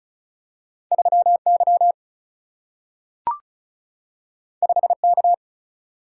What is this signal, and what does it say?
Morse code, heard as a keyed tone, about 36 wpm, 700 Hz. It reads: 3 Y 5 K
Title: Single-Letter Single-Number 35wpm